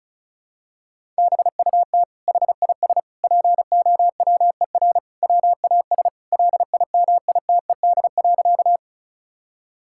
Morse code, heard as a keyed tone, about 34 wpm, 700 Hz. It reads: BUT HIS POWER WAS LIMITED.
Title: Sentences from Top 1000 Words - Review Part 6 - 35wpm